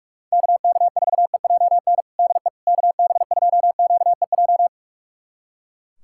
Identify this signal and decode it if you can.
Morse code, heard as a keyed tone, about 42 wpm, 700 Hz. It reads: KK4EJN DE KB2QEJ
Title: USA Callsign DE Callsign 45wpm